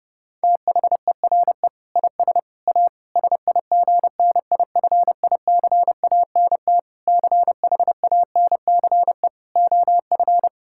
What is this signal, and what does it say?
Morse code, heard as a keyed tone, about 30 wpm, 700 Hz. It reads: THERE IS A SIGNIFICANT CHANCE OF
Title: Sentences from Top 1000 Words - Review Part 6 - 30wpm